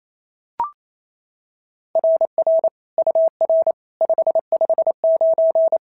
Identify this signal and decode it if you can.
Morse code, heard as a keyed tone, about 29 wpm, 690 Hz. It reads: RR UR 559
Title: SOTA QSO Practice 28wpm